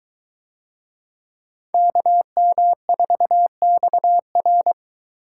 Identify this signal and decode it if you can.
Morse code, heard as a keyed tone, about 21 wpm, 700 Hz. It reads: KM4XR